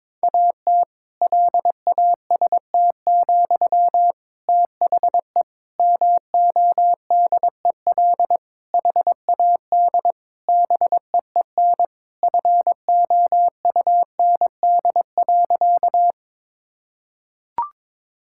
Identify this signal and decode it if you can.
Morse code, heard as a keyed tone, about 22 wpm, 700 Hz. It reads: AT LAST, THE MODEL HAD BEEN FOUND.
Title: Sentences from Top 1000 Words - Review Part 5 - 22wpm